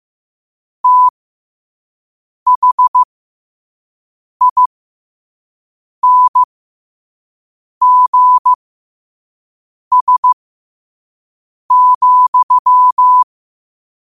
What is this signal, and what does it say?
Morse code, heard as a keyed tone, about 15 wpm, spaced as for 5 wpm, 1000 Hz. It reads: THINGS,
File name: qotdfiles / 5wpm